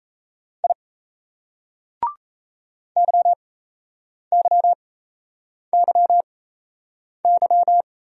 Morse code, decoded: I YYYY

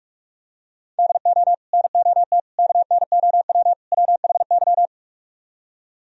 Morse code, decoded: DO NOT KNOW WHY

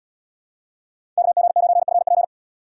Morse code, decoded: KG9FP

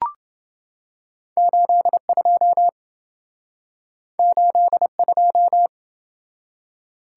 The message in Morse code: 82 82

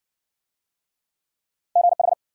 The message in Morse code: GH

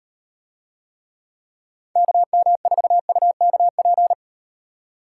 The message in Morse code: KM4UKP

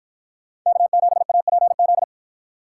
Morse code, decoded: K7APZ

